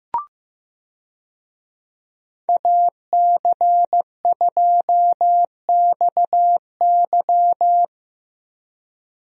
Morse code, heard as a keyed tone, about 15 wpm, 700 Hz. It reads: AC2XY
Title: USA Call Signs - 5 Characters - Repeated 2x - 15wpm